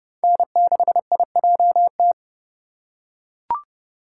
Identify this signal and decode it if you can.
Morse code, heard as a keyed tone, about 31 wpm, 700 Hz. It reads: N6IJT
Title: USA Call Signs - 5 Characters - 3x Character Spacing - 30wpm